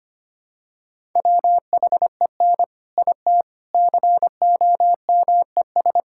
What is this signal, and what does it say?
Morse code, heard as a keyed tone, about 25 wpm, 700 Hz. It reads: WHEN IT COMES
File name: practice / 25wpm / Sentences with 3-Word Phrases - 25wpm